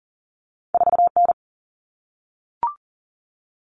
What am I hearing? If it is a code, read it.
Morse code, heard as a keyed tone, about 42 wpm, 700 Hz. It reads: 4N
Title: All Cut Numbers - 2-Digit 40wpm